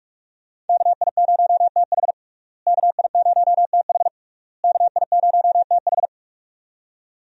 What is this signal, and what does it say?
Morse code, heard as a keyed tone, about 46 wpm, 700 Hz. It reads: KI0TH KI0TH KI0TH
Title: USA Call Signs - 5 Characters - Repeated 3x - 45wpm